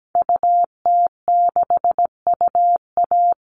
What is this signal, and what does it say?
Morse code, heard as a keyed tone, about 17 wpm, 700 Hz. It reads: UT6UA
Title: Call Signs Repeated 2x - 17wpm